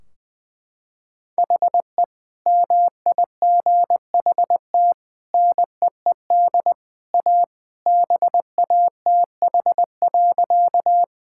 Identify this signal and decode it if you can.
Morse code, heard as a keyed tone, about 20 wpm, 700 Hz. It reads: HE MIGHT NEED A BATH.
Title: Sentences from Top 1600 Words 20wpm